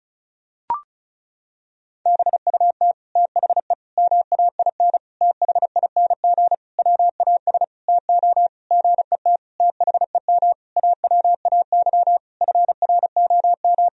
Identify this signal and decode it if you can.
Morse code, heard as a keyed tone, about 35 wpm, 700 Hz. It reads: BUT THE MAIN THING WAS TO GET THEM AWAY FROM
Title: Sentences from Top 1000 Words - Review Part 2 - 35wpm